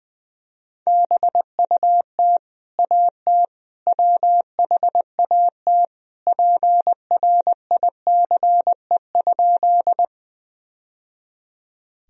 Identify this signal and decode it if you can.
Morse code, heard as a keyed tone, about 21 wpm, 700 Hz. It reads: BUT AT WHAT PRICE?
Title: Sentences from Top 1000 Words - Review Part 4 - 20wpm